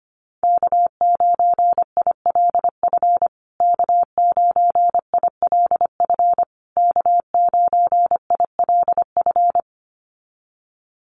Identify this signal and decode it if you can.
Morse code, heard as a keyed tone, about 25 wpm, 700 Hz. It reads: K9ILF K9ILF K9ILF
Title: USA Call Signs Repeated 3x 25wpm